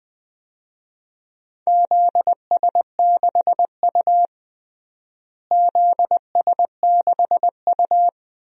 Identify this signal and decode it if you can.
Morse code, heard as a keyed tone, about 20 wpm, 700 Hz. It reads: ZS6U ZS6U
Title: DXCC-100 Call Signs - 4 Characters - Repeated 2x - 20wpm